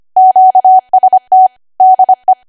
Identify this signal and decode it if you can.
Morse code, heard as a keyed tone, about 25 wpm, 750 Hz. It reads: QST DE
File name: cwa / W1AW-Level3 / 25WPM